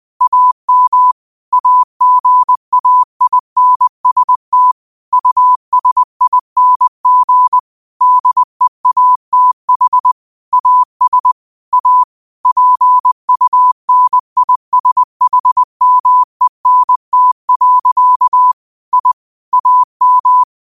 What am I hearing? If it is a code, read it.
Morse code, heard as a keyed tone, about 20 wpm, 1000 Hz. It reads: AM AGAINST USING DEATH AS A PUNISHMENT. I AM